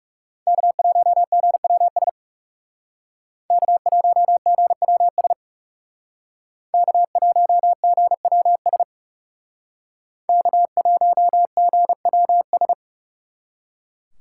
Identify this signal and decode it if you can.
Morse code, heard as a keyed tone, about 39 wpm, 700 Hz. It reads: K1GWS K1GWS K1GWS K1GWS